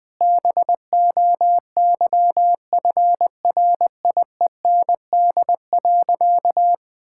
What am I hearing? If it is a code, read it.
Morse code, heard as a keyed tone, about 20 wpm, 700 Hz. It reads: BOYFRIEND.
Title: Sentences from Top 1000 Words - Review Part 6 - 20wpm